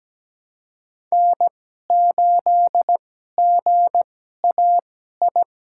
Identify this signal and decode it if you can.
Morse code, heard as a keyed tone, about 17 wpm, spaced as for 12 wpm, 700 Hz. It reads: N8GAI
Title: USA Call Signs - 2x Character Spacing - 17wpm